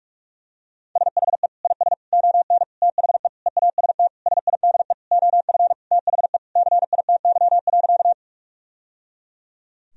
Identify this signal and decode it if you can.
Morse code, heard as a keyed tone, about 47 wpm, 700 Hz. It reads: SHE IS ON THE EAST SIDE OF THE CITY.
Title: Sentences from Top 1000 Words - Review Part 3 - 45wpm